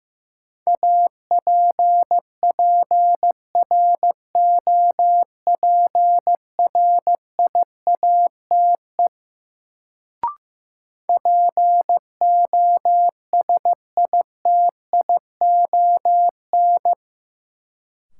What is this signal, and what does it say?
Morse code, heard as a keyed tone, about 16 wpm, 700 Hz. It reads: APPROPRIATE POSITION
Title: Sentences from Top 900 Words - New Word Drill - 15wpm